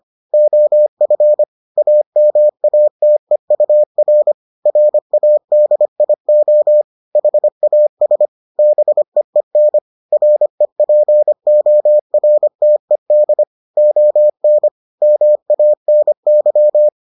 Morse code, heard as a keyed tone, about 25 wpm, 600 Hz. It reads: OF AMATEUR RADIO HAS BEEN REPORTED ON MANY